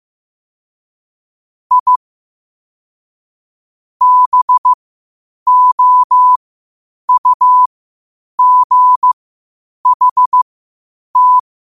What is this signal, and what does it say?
Morse code, heard as a keyed tone, about 15 wpm, spaced as for 8 wpm, 1000 Hz. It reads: I BOUGHT
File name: qotdfiles / 8wpm